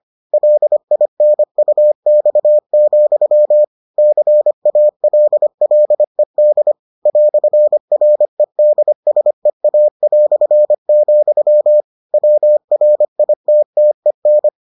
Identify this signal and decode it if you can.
Morse code, heard as a keyed tone, about 25 wpm, 600 Hz. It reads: LINUX, CALLED "REDSEA", WRITTEN